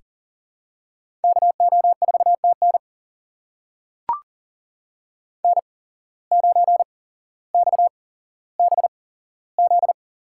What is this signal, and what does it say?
Morse code, heard as a keyed tone, about 40 wpm, spaced as for 17 wpm, 700 Hz. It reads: KO4TN N9XBZ
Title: USA Call Signs - 5 Characters - 8x Character Spacing - 40wpm